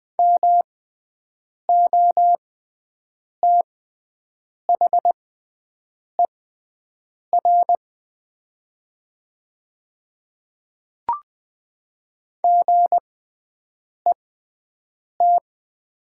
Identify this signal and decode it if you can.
Morse code, heard as a keyed tone, about 20 wpm, spaced as for 7 wpm, 700 Hz. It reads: MOTHER GET
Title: 6x Character Spacing - Top 200 Words - 20wpm